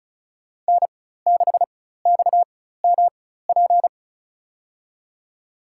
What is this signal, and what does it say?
Morse code, heard as a keyed tone, about 34 wpm, spaced as for 16 wpm, 700 Hz. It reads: N6XMP
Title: USA Call Signs - 5 Characters - 4x Character Spacing - 35wpm